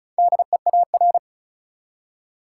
Morse code, heard as a keyed tone, about 36 wpm, 700 Hz. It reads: DEAR
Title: Four Letter Words - 35wpm